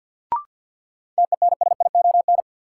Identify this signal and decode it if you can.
Morse code, heard as a keyed tone, about 50 wpm, 700 Hz. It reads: TENSION
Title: Sentences from Top 1600 Words - New Word Drill Alt - 50wpm